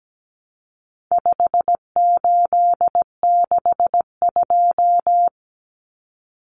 Morse code, heard as a keyed tone, about 17 wpm, 700 Hz. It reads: 5862
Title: Common Cut Numbers - 4-Digit 17wpm